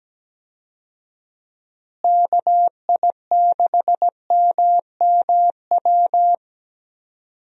Morse code, heard as a keyed tone, about 17 wpm, 700 Hz. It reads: KI6MMW
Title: USA Call Signs - 6 Characters - Repeated 2x - 17wpm